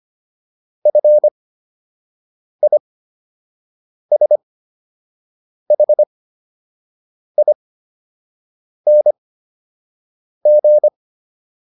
Morse code, heard as a keyed tone, about 25 wpm, spaced as for 6 wpm, 600 Hz. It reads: FISHING